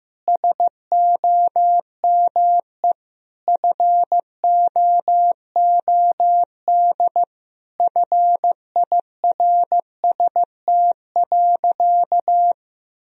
Morse code, heard as a keyed tone, about 15 wpm, 700 Hz. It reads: SOME FOOD FIRST.
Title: Sentences from Top 1000 Words - Review Part 1 - 15wpm